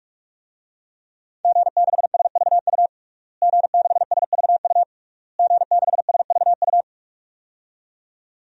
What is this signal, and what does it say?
Morse code, heard as a keyed tone, about 46 wpm, 700 Hz. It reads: G6SVU G6SVU G6SVU
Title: DXCC-100 Call Signs - 5 Characters - Repeated 3x - 45wpm